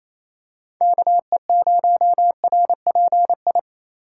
Morse code, heard as a keyed tone, about 28 wpm, 700 Hz. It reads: KE0RPI
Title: USA Call Signs - 6 Characters - 3x Character Spacing - 28wpm